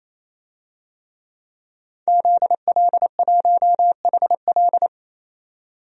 Morse code, heard as a keyed tone, about 28 wpm, 700 Hz. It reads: ZL1HL
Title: DXCC-100 Call Signs - 5 Characters - 28wpm